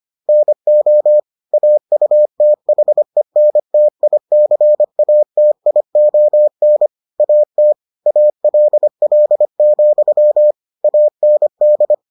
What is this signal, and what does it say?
Morse code, heard as a keyed tone, about 25 wpm, 600 Hz. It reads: NO AUTHENTICATION AT ALL, AND